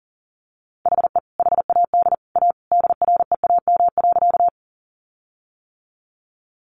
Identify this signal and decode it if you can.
Morse code, heard as a keyed tone, about 40 wpm, 700 Hz. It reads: HE HAD A DREAM.
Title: Sentences from Top 1500 Words 40wpm